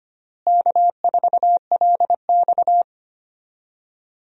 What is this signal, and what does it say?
Morse code, heard as a keyed tone, about 25 wpm, 700 Hz. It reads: K4LX
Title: USA Call Signs - 4 Characters - Repeated 3x - 25wpm